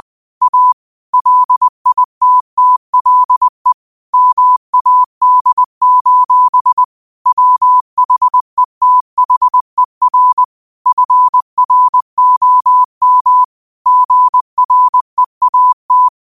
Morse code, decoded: A LITTLE MAD: WHETHER FROM GREAT

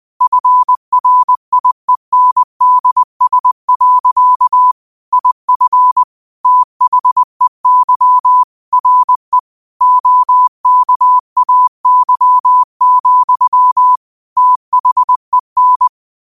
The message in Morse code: FRIENDS. IF THEY RE OKAY, THEN

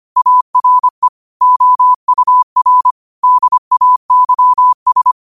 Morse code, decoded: ARE OUR DAYS

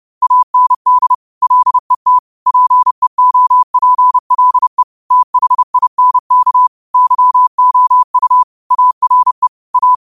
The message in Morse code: AND LET PEOPLE THINK YOU ARE A

